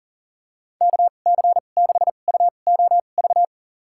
Morse code, heard as a keyed tone, about 40 wpm, spaced as for 29 wpm, 700 Hz. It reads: KC6UOV
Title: USA Call Signs - 6 Characters - 2x Character Spacing - 40wpm